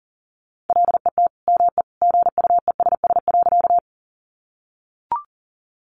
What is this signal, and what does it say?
Morse code, heard as a keyed tone, about 40 wpm, 700 Hz. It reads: LET ME GUESS.
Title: Sentences from Top 1500 Words 40wpm